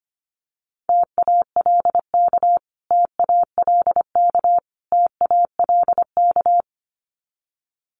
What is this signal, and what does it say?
Morse code, heard as a keyed tone, about 25 wpm, 700 Hz. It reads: TALK TALK TALK
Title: Top 500 Words Repeated 25wpm